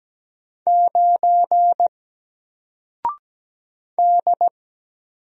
Morse code, heard as a keyed tone, about 17 wpm, 700 Hz. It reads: 9 D
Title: Single Character (Letter:Number:Punctuation:Prosign) Alt 17wpm